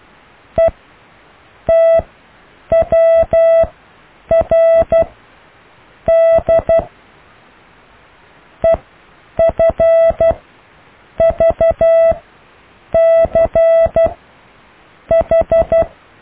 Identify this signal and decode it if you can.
Morse code, heard as a keyed tone, about 12 wpm, spaced as for 7 wpm, 660 Hz. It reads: ETWRD EFVCH
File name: cw / IJLUM+CDRO+EKTPV+BFHNW